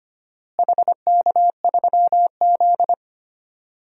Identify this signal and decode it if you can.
Morse code, heard as a keyed tone, about 25 wpm, 700 Hz. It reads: HK3Z
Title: DXCC-100 Call Signs - 4 Characters - Repeated 2x - 25wpm